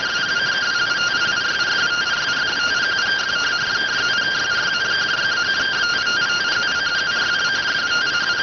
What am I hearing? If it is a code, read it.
Two-tone teleprinter signal, about 170 Hz shift.